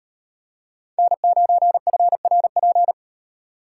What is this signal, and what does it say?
Morse code, heard as a keyed tone, about 35 wpm, 700 Hz. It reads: N9FRP